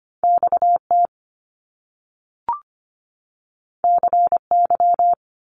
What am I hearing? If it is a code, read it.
Morse code, heard as a keyed tone, about 25 wpm, 700 Hz. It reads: X T C Y
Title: Top Two Letter Combinations in Words 25wpm